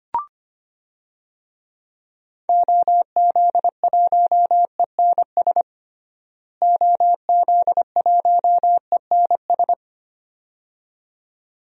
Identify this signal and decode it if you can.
Morse code, heard as a keyed tone, about 25 wpm, 700 Hz. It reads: OZ1ENS OZ1ENS
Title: DXCC-100 Call Signs - 6 Characters - Repeated 2x - 25wpm